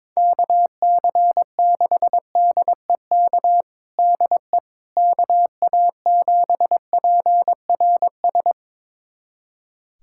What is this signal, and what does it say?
Morse code, heard as a keyed tone, about 22 wpm, 700 Hz. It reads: KC6DEK DE KA7PRS
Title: USA Callsign DE Callsign - 6 Characters - 22wpm